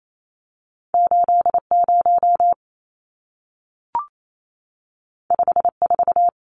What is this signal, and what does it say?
Morse code, heard as a keyed tone, about 28 wpm, 700 Hz. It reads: 80 54
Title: Numbers - 2-Digit 28wpm